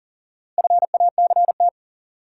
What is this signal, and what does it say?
Morse code, heard as a keyed tone, about 40 wpm, 700 Hz. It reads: FACT
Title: Sets of 2 Words - New Word Drill - 40wpm